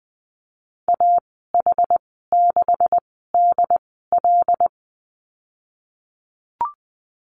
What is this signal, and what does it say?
Morse code, heard as a keyed tone, about 20 wpm, spaced as for 15 wpm, 700 Hz. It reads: AH6DL
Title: 154 Lesson 25g - Letter p - Call Signs 20wpm